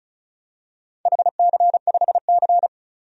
Morse code, heard as a keyed tone, about 36 wpm, 700 Hz. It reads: HC5C
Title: DXCC-100 Call Signs - 4 Characters - 35wpm